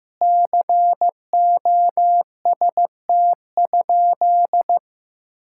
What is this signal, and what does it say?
Morse code, heard as a keyed tone, about 15 wpm, 700 Hz. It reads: COST?
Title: Sentences from Top 1000 Words - Review Part 6 - 15wpm